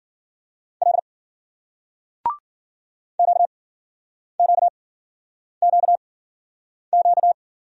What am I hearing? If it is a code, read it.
Morse code, heard as a keyed tone, about 54 wpm, spaced as for 9 wpm, 700 Hz. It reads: L QQQQ